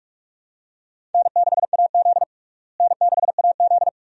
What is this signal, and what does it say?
Morse code, heard as a keyed tone, about 47 wpm, 700 Hz. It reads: N6AZ N6AZ
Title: USA Call Signs - 4 Characters - Repeated 2x - 45wpm